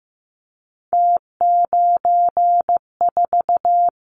Morse code, heard as a keyed tone, about 15 wpm, 700 Hz. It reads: T94
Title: Callsign Prefixes for Top 50 DXCC - 15wpm